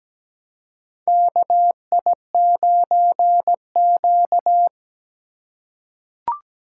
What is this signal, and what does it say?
Morse code, heard as a keyed tone, about 17 wpm, 700 Hz. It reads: KI9Q
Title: USA Call Signs - 4 Characters - 3x Character Spacing - 17wpm